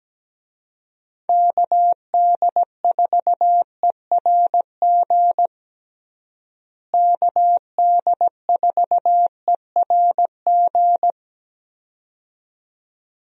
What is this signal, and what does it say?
Morse code, heard as a keyed tone, about 17 wpm, 700 Hz. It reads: KD4ERG KD4ERG
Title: USA Call Signs - 6 Characters - Repeated 2x - 17wpm